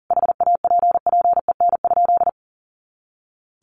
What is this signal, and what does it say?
Morse code, heard as a keyed tone, about 40 wpm, 700 Hz. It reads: HAPPEN?
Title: Sentences from Top 400 Words 40wpm